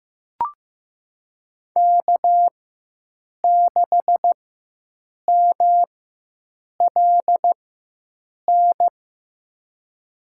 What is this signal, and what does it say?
Morse code, heard as a keyed tone, about 15 wpm, spaced as for 7 wpm, 700 Hz. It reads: K6MLN
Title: USA Call Signs - 4x Character Spacing - 15wpm